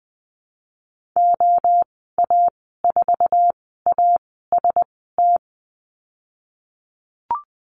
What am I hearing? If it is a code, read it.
Morse code, heard as a keyed tone, about 20 wpm, spaced as for 15 wpm, 700 Hz. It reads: OA4AST
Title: 122 Lesson 20g - Number 3 - Call Signs 20wpm